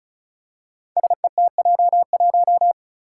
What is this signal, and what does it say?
Morse code, heard as a keyed tone, about 36 wpm, 700 Hz. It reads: SETJ1
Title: Five Challenging Letter-Number Combinations 35wpm